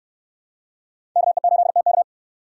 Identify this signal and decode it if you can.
Morse code, heard as a keyed tone, about 68 wpm, 700 Hz. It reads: KE8TK